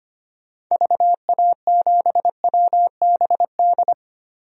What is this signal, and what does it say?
Morse code, heard as a keyed tone, about 25 wpm, 700 Hz. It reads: VA7WBD
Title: DXCC-100 Call Signs - 6 Characters - 25wpm